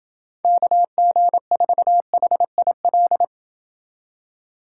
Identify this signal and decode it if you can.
Morse code, heard as a keyed tone, about 26 wpm, 700 Hz. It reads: KG4HIL